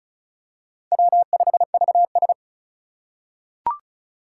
Morse code, heard as a keyed tone, about 36 wpm, 700 Hz. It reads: W5VS
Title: USA Call Signs - 4x Character Spacing - 35wpm